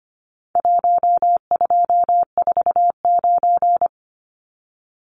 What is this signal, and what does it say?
Morse code, heard as a keyed tone, about 25 wpm, 700 Hz. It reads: 1249
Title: Numbers - 4-Digit 25wpm